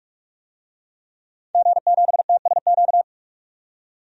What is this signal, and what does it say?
Morse code, heard as a keyed tone, about 46 wpm, 700 Hz. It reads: G7TSQ
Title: DXCC-100 Call Signs - 5 Characters - 45wpm